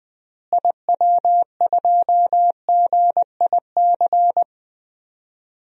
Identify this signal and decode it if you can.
Morse code, heard as a keyed tone, about 20 wpm, 700 Hz. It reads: IW2GIC
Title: DXCC-100 Call Signs - 6 Characters - Repeated 2x - 20wpm